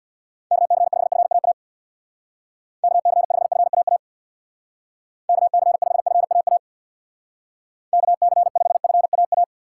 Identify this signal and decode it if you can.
Morse code, heard as a keyed tone, about 65 wpm, 700 Hz. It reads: KC5FAA KC5FAA KC5FAA KC5FAA